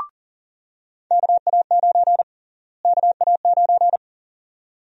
Morse code, heard as a keyed tone, about 40 wpm, 700 Hz. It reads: KA9 KA9
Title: USA Call Sign Prefixes - Repeated 2x - 40wpm